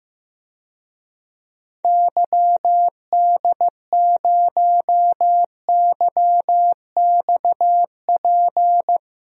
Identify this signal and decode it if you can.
Morse code, heard as a keyed tone, about 15 wpm, 700 Hz. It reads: YD0YXP
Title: DXCC-100 Call Signs - 6 Characters - Repeated 2x - 15wpm